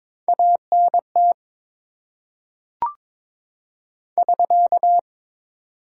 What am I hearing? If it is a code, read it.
Morse code, heard as a keyed tone, about 22 wpm, 700 Hz. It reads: ANT <SK>
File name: Top 90 QSO Elements - 4x Character Spacing - 22wpm